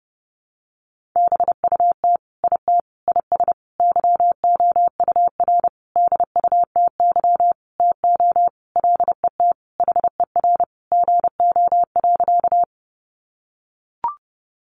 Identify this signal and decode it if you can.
Morse code, heard as a keyed tone, about 30 wpm, 700 Hz. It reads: BUT IT IS YOUR DUTY TO LET HER GO.
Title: Sentences from Top 1500 Words 30wpm